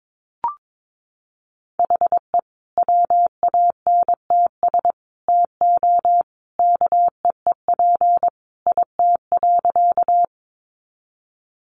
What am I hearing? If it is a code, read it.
Morse code, heard as a keyed tone, about 22 wpm, 700 Hz. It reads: HE WANTS TO KEEP IT.
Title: Additional Easy Phrases 22wpm